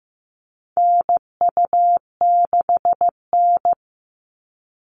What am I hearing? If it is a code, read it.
Morse code, heard as a keyed tone, about 15 wpm, 700 Hz. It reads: NU6N
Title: Call Signs Repeated 3x - 15wpm